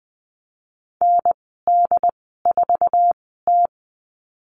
Morse code, heard as a keyed tone, about 20 wpm, spaced as for 14 wpm, 700 Hz. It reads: ND4T
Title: 076 Lesson 13g - Letter L - Call Signs 20wpm